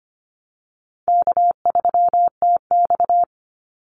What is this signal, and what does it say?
Morse code, heard as a keyed tone, about 25 wpm, 700 Hz. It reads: K3TX
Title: Easy Call Signs Repeated 3x - 25wpm